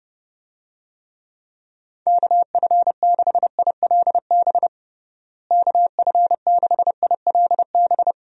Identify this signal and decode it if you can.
Morse code, heard as a keyed tone, about 31 wpm, 700 Hz. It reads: KF6ILB KF6ILB
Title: USA Call Signs - 6 Characters - Repeated 2x - 30wpm